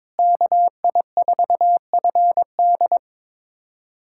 Morse code, heard as a keyed tone, about 22 wpm, 700 Hz. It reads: KI4FD
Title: USA Callsign DE Callsign - 5 Characters - 22wpm